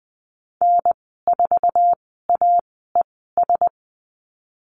Morse code, heard as a keyed tone, about 20 wpm, spaced as for 15 wpm, 700 Hz. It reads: N4AES
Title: 062 Lesson 11g - Letter h - Call Signs 20wpm